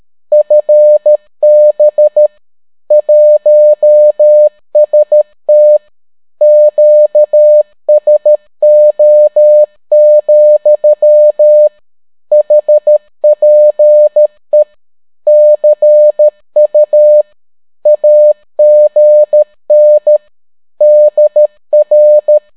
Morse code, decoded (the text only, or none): FB 1ST QSO, HPE CU AGN DR